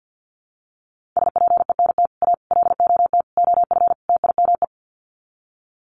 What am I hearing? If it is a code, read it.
Morse code, heard as a keyed tone, about 48 wpm, 700 Hz. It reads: SPENT A LOT OF TIME
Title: Sets of 5 Words 50wpm